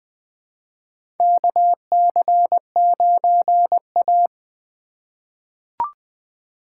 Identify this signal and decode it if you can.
Morse code, heard as a keyed tone, about 20 wpm, 700 Hz. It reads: KC9A
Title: USA Call Signs - 4 Characters - 4x Character Spacing - 20wpm